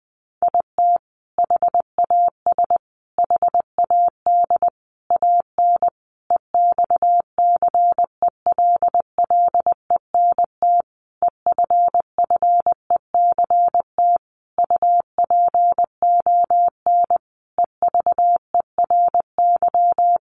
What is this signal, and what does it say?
Morse code, heard as a keyed tone, about 20 wpm, 700 Hz. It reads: IT HAS HAD AN EXCELLENT EFFECT UPON EVERY
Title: Sentences from Top 1100 Words 20wpm